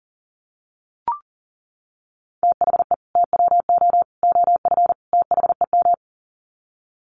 Morse code, heard as a keyed tone, about 40 wpm, 700 Hz. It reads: THE TWO OF THEM
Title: Sets of 4 Words 40wpm